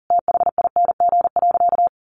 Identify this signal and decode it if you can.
Morse code, heard as a keyed tone, about 40 wpm, 700 Hz. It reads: THING.
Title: Sentences from Top 1100 Words 40wpm